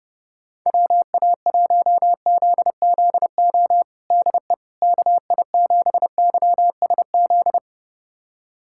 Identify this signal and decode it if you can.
Morse code, heard as a keyed tone, about 31 wpm, 700 Hz. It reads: WA1ZZO DE KI7YSZ
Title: USA Callsign DE Callsign - 6 Characters - 30wpm